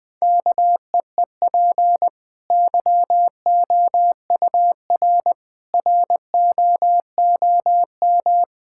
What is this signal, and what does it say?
Morse code, heard as a keyed tone, about 20 wpm, 700 Hz. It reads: KEEP YOUR ROOM
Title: Easy Sentences with Hints - 20wpm